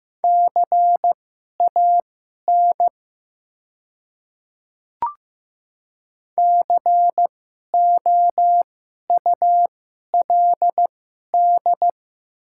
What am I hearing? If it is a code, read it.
Morse code, heard as a keyed tone, about 15 wpm, spaced as for 11 wpm, 700 Hz. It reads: CAN COULD
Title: 2x Character Spacing - Top 100 Words - 15wpm